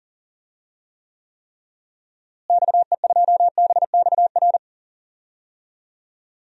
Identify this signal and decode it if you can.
Morse code, heard as a keyed tone, about 40 wpm, 700 Hz. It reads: XE2BXR
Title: DXCC-100 Call Signs - 6 Characters - Repeated 3x - 40wpm